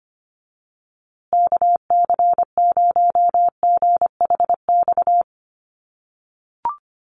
Morse code, heard as a keyed tone, about 25 wpm, 700 Hz. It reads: KC0GHX
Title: USA Call Signs Repeated 3x 25wpm